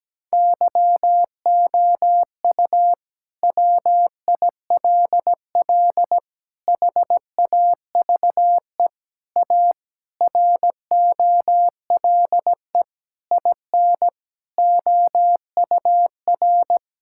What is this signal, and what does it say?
Morse code, heard as a keyed tone, about 17 wpm, 700 Hz. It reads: YOU WILL HAVE A ROLE IN OUR
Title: Sentences from Top 1000 Words - Review Part 6 - 17wpm